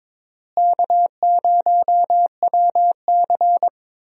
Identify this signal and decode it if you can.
Morse code, heard as a keyed tone, about 22 wpm, 700 Hz. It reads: K0WC
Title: USA Call Signs - 4 Characters - Repeated 3x - 22wpm